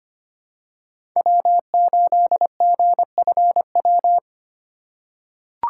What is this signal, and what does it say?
Morse code, heard as a keyed tone, about 25 wpm, 700 Hz. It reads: W8GFW
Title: USA Call Signs - 5 Characters - 6x Character Spacing - 25wpm